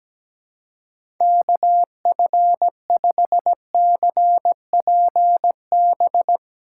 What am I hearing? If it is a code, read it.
Morse code, heard as a keyed tone, about 17 wpm, 700 Hz. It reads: KF5CPB
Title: USA Call Signs - 6 Characters - 3x Character Spacing - 17wpm